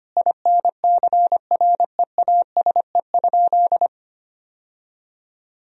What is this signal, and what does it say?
Morse code, heard as a keyed tone, about 25 wpm, 700 Hz. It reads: INCREASE?
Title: Sentences from Top 1000 Words - Review Part 4 - 25wpm